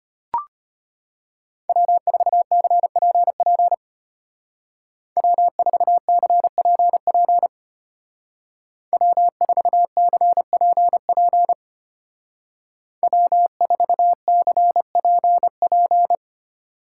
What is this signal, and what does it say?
Morse code, heard as a keyed tone, about 31 wpm, 700 Hz. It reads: W4CPP W4CPP W4CPP W4CPP
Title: Speed Racing - USA 1x3 Call Signs 25wpm